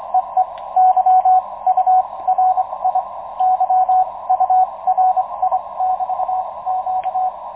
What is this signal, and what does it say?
Morse code, heard as a keyed tone, about 25 wpm, spaced as for 21 wpm, 740 Hz, with radio noise. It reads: EE Y U RI Y URI <BT> Q
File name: Expedition 2019 / ua1ceg-p 10.22.2019 09.36 utc audio DL1HSI